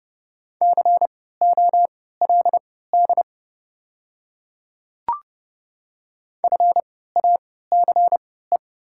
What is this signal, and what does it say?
Morse code, heard as a keyed tone, about 31 wpm, spaced as for 17 wpm, 700 Hz. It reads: COLD FACE
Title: 3x Character Spacing - Top 500 Words - 30wpm